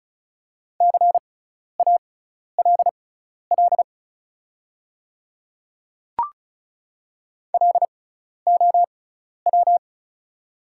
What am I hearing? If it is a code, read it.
Morse code, heard as a keyed tone, about 36 wpm, spaced as for 12 wpm, 700 Hz. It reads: CALL LOW
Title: 6x Character Spacing - Top 200 Words - 35wpm